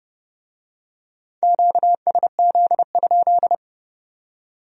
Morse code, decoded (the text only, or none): QSZ?